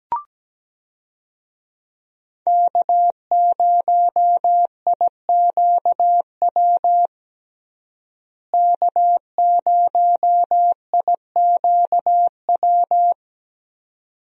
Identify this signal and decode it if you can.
Morse code, heard as a keyed tone, about 17 wpm, 700 Hz. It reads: K0IQW K0IQW
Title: USA Call Signs - 5 Characters - Repeated 2x - 17wpm